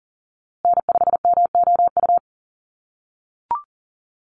Morse code, heard as a keyed tone, about 40 wpm, 700 Hz. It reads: N5MOU
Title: USA Call Signs Repeated 3x 40wpm